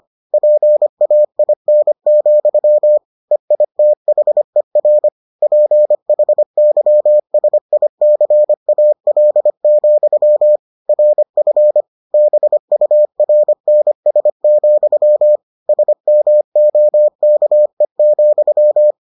Morse code, heard as a keyed tone, about 25 wpm, 600 Hz. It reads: PAIN, EITHER PHYSICAL, RF BURNS, SMOKE,